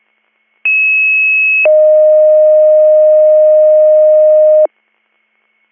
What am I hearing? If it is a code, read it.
Two-tone page, heard as a one-second tone, then a three-second tone, about 2600 then 620 Hz. It is heard from 0.6 to 4.7 s.